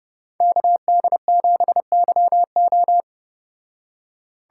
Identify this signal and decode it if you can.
Morse code, heard as a keyed tone, about 33 wpm, 700 Hz. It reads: KD7YO